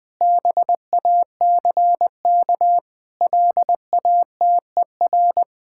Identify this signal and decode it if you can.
Morse code, heard as a keyed tone, about 20 wpm, 700 Hz. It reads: BACK LATER
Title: Sentences with 4-Word Phrases - 20wpm